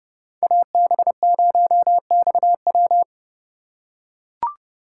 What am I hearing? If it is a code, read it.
Morse code, heard as a keyed tone, about 29 wpm, 700 Hz. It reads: AB0XW